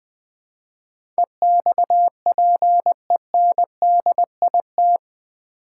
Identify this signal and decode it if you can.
Morse code, heard as a keyed tone, about 20 wpm, 700 Hz. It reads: EXPENDIT
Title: Build A Word - 4 Letters At A Time - 11-Letter Words - 20wpm